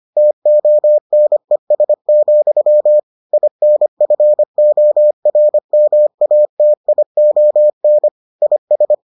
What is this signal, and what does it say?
Morse code, heard as a keyed tone, about 25 wpm, 600 Hz. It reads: TONES, INFORMATION IS